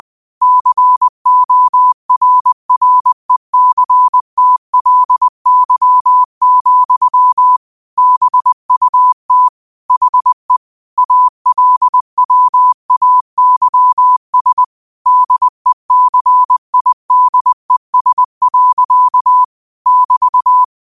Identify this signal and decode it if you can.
Morse code, heard as a keyed tone, about 20 wpm, 1000 Hz. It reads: CORRECTLY, BUT HE ALWAYS DECIDES. <BT>
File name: qotdfiles / 20wpm